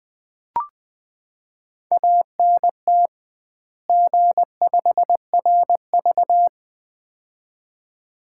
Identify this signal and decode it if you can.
Morse code, heard as a keyed tone, about 20 wpm, 700 Hz. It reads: ANT G5RV
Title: Common QSO Phrases - 2x Word-Spacing - 20wpm